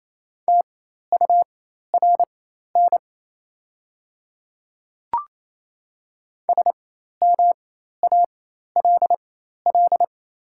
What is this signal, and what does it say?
Morse code, heard as a keyed tone, about 28 wpm, spaced as for 13 wpm, 700 Hz. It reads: TURN SMALL
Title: 4x Character Spacing - Top 300 Words - 28wpm